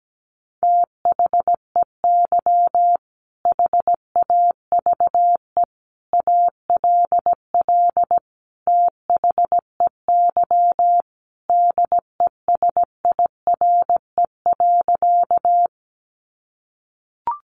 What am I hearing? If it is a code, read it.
Morse code, heard as a keyed tone, about 17 wpm, 700 Hz. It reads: THEY HAVE ALL THEY DESIRE.
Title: Sentences from Top 1100 Words 17wpm